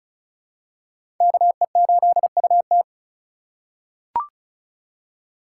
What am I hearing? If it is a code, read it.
Morse code, heard as a keyed tone, about 36 wpm, 700 Hz. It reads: KE8UT
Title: USA Call Signs - 5 Characters - 3x Character Spacing - 35wpm